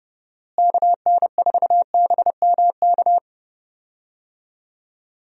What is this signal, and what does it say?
Morse code, heard as a keyed tone, about 30 wpm, 700 Hz. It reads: KN4BMK
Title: USA Call Signs - 6 Characters - Repeated 3x - 30wpm